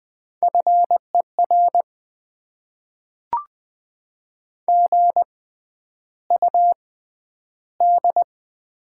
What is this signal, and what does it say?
Morse code, heard as a keyed tone, about 20 wpm, 700 Hz. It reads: F E R G U D
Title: Top 20 QSO Elements - 6x Character Spacing - 20wpm